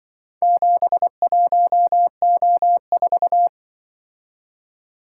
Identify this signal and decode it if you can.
Morse code, heard as a keyed tone, about 24 wpm, 700 Hz. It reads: 71O4
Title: Speed-Racing - Four Letter-Number Combinations - 20wpm